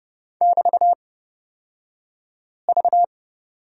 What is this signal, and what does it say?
Morse code, heard as a keyed tone, about 29 wpm, spaced as for 5 wpm, 690 Hz. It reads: <BT>V
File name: Lesson 36d - BT - Two Characters 30wpm